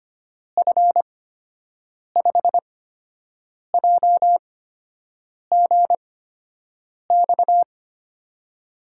Morse code, heard as a keyed tone, about 25 wpm, spaced as for 7 wpm, 700 Hz. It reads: F5JGX
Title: USA Call Signs - 6 Characters - 8x Character Spacing - 25wpm